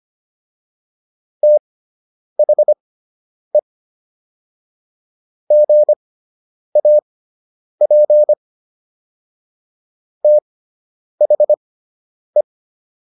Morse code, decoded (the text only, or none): THE GAP THE